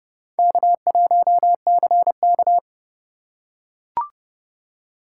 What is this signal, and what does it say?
Morse code, heard as a keyed tone, about 28 wpm, 700 Hz. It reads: K1CK